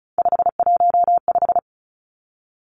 Morse code, heard as a keyed tone, about 36 wpm, 700 Hz. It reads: 515
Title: Three Challenging Letter-Number Combinations 35wpm